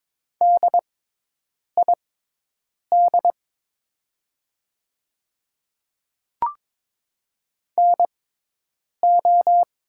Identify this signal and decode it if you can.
Morse code, heard as a keyed tone, about 22 wpm, spaced as for 8 wpm, 700 Hz. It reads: DID NO
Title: 6x Character Spacing - Top 300 Words - 22wpm